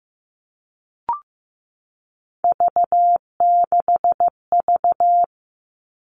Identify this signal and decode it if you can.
Morse code, heard as a keyed tone, about 15 wpm, 700 Hz. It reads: V6V
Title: All Cut Numbers - 3-Digit 15wpm